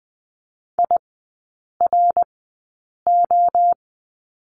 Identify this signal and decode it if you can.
Morse code, heard as a keyed tone, about 20 wpm, spaced as for 8 wpm, 700 Hz. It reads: IRO
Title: 053 Lesson 10e - Letter r - Three Characters 20wpm